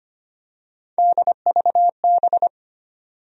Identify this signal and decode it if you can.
Morse code, heard as a keyed tone, about 25 wpm, 700 Hz. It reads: DVB
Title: USA Call Sign Postfixes - 3 Letters 25wpm